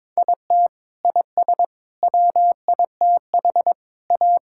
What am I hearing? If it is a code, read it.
Morse code, heard as a keyed tone, about 21 wpm, 700 Hz. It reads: IT IS WITH A
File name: practice / 22wpm / Sets of 7 Words - 22wpm